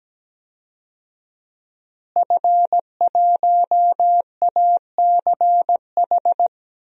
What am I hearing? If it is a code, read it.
Morse code, heard as a keyed tone, about 17 wpm, 700 Hz. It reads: F1ACH
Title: DXCC-100 Call Signs - 5 Characters - Repeated 2x - 17wpm